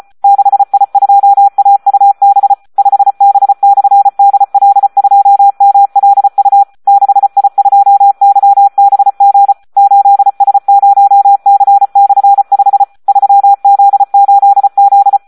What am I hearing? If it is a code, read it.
Morse code, heard as a keyed tone, about 34 wpm, 800 Hz. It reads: 6I2AUB 5B/DL2MLU 6I2YBG 8S0C/5 3Z8Z
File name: Audio practica CW / SD 1 34